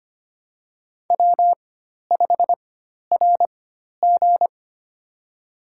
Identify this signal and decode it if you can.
Morse code, heard as a keyed tone, about 25 wpm, spaced as for 12 wpm, 700 Hz. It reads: W5RG
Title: USA Call Signs - 4x Character Spacing - 25wpm